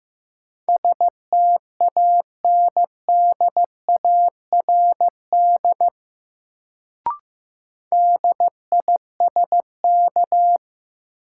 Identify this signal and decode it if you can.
Morse code, heard as a keyed tone, about 15 wpm, 700 Hz. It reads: STANDARD DISK
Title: Sentences from Top 1000 Words - Review Part 6 - New Word Drill - 15wpm